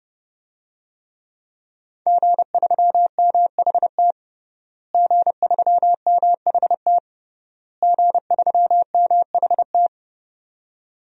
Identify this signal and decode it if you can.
Morse code, heard as a keyed tone, about 32 wpm, 700 Hz. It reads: G3MHT G3MHT G3MHT
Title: DXCC-100 Call Signs - 5 Characters - Repeated 3x - 30wpm